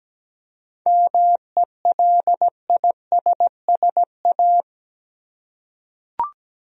Morse code, heard as a keyed tone, about 17 wpm, 700 Hz. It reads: MELISSA
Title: USA Common Names - 17wpm